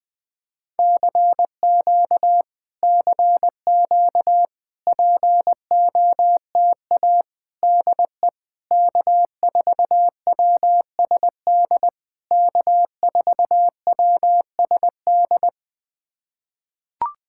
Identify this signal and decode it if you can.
Morse code, heard as a keyed tone, about 20 wpm, 700 Hz. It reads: CQ CQ POTA DE K4WSD K4WSD
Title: POTA QSO Practice - 20wpm